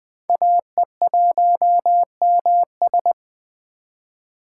Morse code, heard as a keyed tone, about 21 wpm, 700 Hz. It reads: AE1MS
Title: 2023 SOTA Chaser Call Signs - 20wpm